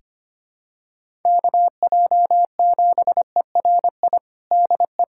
Morse code, heard as a keyed tone, about 25 wpm, 700 Hz. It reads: KJ7ERI DE
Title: USA Callsign DE Callsign 25wpm